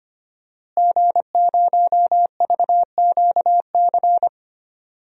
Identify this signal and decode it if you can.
Morse code, heard as a keyed tone, about 25 wpm, 700 Hz. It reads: G0VQC
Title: DXCC-100 Call Signs - 5 Characters - Repeated 2x - 25wpm